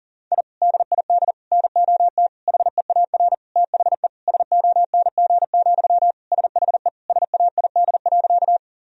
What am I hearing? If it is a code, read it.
Morse code, heard as a keyed tone, about 40 wpm, 700 Hz. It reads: I DID NOT HEAR THE SONG, SHE SAID.
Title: Sentences from Top 1000 Words - Review Part 2 - 40wpm